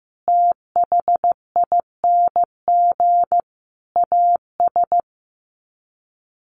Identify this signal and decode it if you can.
Morse code, heard as a keyed tone, about 15 wpm, 700 Hz. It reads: THING AS
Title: Sets of 5 Words Encore 15wpm